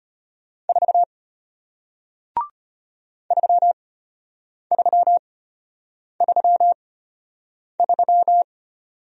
Morse code, heard as a keyed tone, about 33 wpm, spaced as for 8 wpm, 700 Hz. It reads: 4 3333